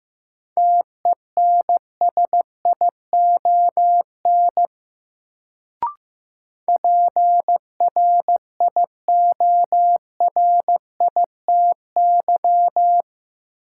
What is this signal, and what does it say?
Morse code, heard as a keyed tone, about 15 wpm, 700 Hz. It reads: TENSION PRIORITY
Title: Sentences from Top 1600 Words - New Word Drill - 15wpm